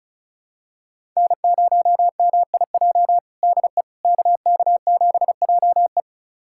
Morse code, heard as a keyed tone, about 36 wpm, 700 Hz. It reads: N0MIJ DE KK7JE
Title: USA Callsign DE Callsign - 5 Characters - 35wpm